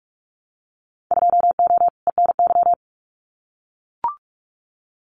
Morse code, heard as a keyed tone, about 44 wpm, 700 Hz. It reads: 2O ENY
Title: Winter Field Day Exchanges 45wpm